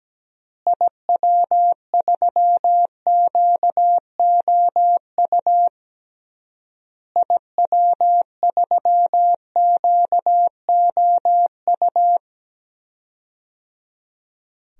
Morse code, heard as a keyed tone, about 17 wpm, 700 Hz. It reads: IW3QOU IW3QOU
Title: DXCC-100 Call Signs - 6 Characters - Repeated 3x - 17wpm